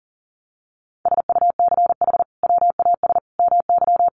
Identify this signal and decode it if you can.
Morse code, heard as a keyed tone, about 40 wpm, 700 Hz. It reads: SUCH WAS MY
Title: Sentences from Top 800 Words 40wpm